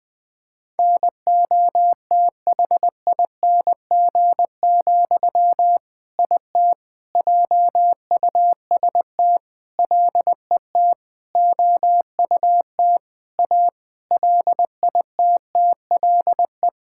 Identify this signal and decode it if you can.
Morse code, heard as a keyed tone, about 20 wpm, 700 Hz. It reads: NOTHING, IT JUST LET OUT A LITTLE
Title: All-Age Appropriate Jokes - 20wpm